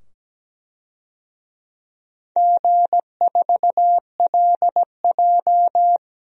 Morse code, heard as a keyed tone, about 17 wpm, 700 Hz. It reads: G4LJ
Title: DXCC-100 Call Signs - 4 Characters - 17wpm